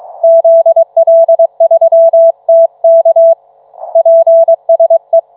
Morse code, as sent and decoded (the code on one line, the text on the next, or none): --.. .-.. ...-- - -.- / .--. ... .
ZL3TK PSE